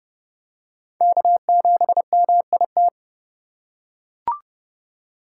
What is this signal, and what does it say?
Morse code, heard as a keyed tone, about 30 wpm, 700 Hz. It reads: K7MIT
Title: USA Call Signs - 5 Characters - 2x Character Spacing - 30wpm